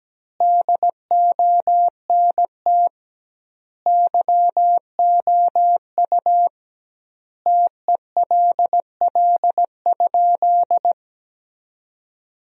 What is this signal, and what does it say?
Morse code, heard as a keyed tone, about 17 wpm, 700 Hz. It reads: DONT YOU TELL?
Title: 2x Word-Spacing - Sentences from Top 300 Words - 17wpm